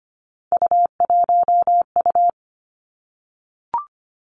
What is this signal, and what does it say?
Morse code, heard as a keyed tone, about 25 wpm, 700 Hz. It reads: U1U
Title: Three Challenging Letter-Number Combinations 25wpm